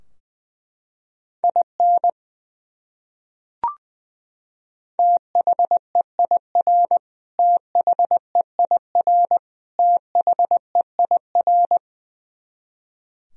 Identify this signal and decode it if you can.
Morse code, heard as a keyed tone, about 20 wpm, 700 Hz. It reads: IN THEIR THEIR THEIR
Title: Top 100 Words Repeated 20wpm